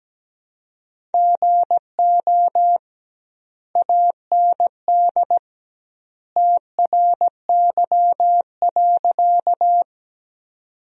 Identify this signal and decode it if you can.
Morse code, heard as a keyed tone, about 17 wpm, 700 Hz. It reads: GO AND TRY.
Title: 2x Word-Spacing - Sentences from Top 200 Words - 17wpm